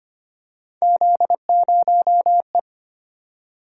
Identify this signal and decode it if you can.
Morse code, heard as a keyed tone, about 25 wpm, 700 Hz. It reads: Z0E
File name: Three Letter-Number Combinations - 25wpm